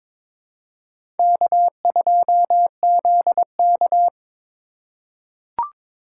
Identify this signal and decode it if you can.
Morse code, heard as a keyed tone, about 21 wpm, 700 Hz. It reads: K2ZK
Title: USA Call Signs - 4 Characters - 4x Character Spacing - 22wpm